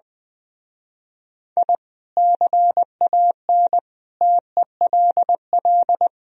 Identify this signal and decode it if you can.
Morse code, heard as a keyed tone, about 21 wpm, 700 Hz. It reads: I CAN TELL
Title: Speed-Racing - Sets of 5 Words 20wpm